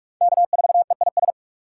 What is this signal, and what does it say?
Morse code, heard as a keyed tone, about 44 wpm, 700 Hz. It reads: K4EIS